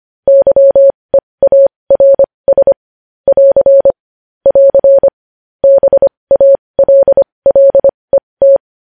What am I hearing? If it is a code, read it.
Morse code, heard as a keyed tone, about 25 wpm, spaced as for 20 wpm, 550 Hz. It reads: YEARS <AR> <AR> BALLET
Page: News Headlines in Morse Code at 20 WPM